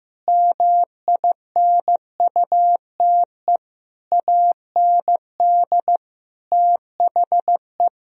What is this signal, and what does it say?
Morse code, heard as a keyed tone, about 15 wpm, 700 Hz. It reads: MINUTE AND THE
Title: Sentences from Top 1000 Words - Review Part 3 - 15wpm